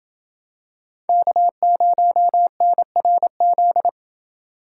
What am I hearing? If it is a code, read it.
Morse code, heard as a keyed tone, about 28 wpm, 700 Hz. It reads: K0NRZ